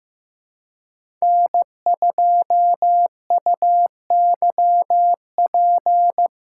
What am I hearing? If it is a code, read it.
Morse code, heard as a keyed tone, about 16 wpm, 700 Hz. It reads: N2UYP
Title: USA Call Signs - 3x Character Spacing - 15wpm